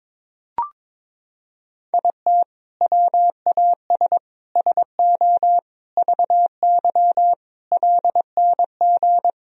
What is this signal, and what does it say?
Morse code, heard as a keyed tone, about 22 wpm, 700 Hz. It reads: IT WAS SO VY LNG
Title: Sentences from Top 500 Words and Abbreviations 22wpm